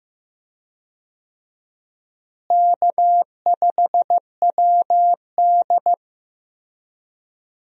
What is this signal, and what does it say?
Morse code, heard as a keyed tone, about 15 wpm, 700 Hz. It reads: K5WD
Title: USA Call Signs - 4 Characters - Repeated 3x - 15wpm